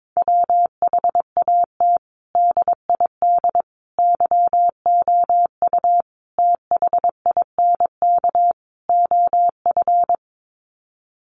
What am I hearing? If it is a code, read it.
Morse code, heard as a keyed tone, about 22 wpm, 700 Hz. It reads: WHAT DID YOU THINK OF
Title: Sets of 5 Words Encore2 22wpm